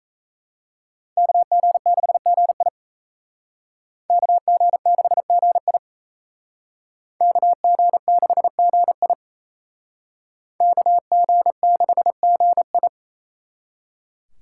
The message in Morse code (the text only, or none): KG6GI KG6GI KG6GI KG6GI